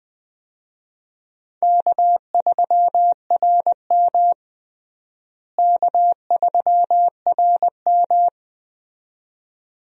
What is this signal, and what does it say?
Morse code, heard as a keyed tone, about 20 wpm, 700 Hz. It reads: K3RM K3RM
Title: USA Call Signs - 4 Characters - Repeated 2x - 20wpm